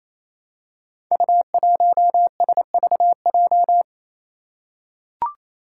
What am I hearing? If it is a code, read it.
Morse code, heard as a keyed tone, about 28 wpm, 700 Hz. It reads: U1SVJ
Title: Five Challenging Letter-Number Combinations 28wpm